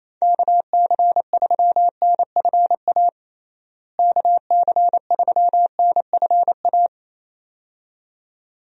Morse code, heard as a keyed tone, about 28 wpm, 700 Hz. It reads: KC3NFA KC3NFA
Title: USA Call Signs - 6 Characters - Repeated 2x - 28wpm